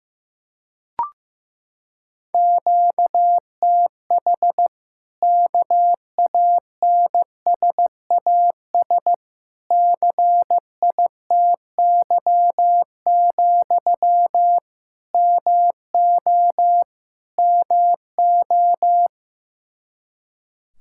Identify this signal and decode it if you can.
Morse code, heard as a keyed tone, about 15 wpm, 700 Hz. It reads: QTH KANSAS CITY, MO MO
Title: Practice QSO - Exchange 1 - Beginning - 15wpm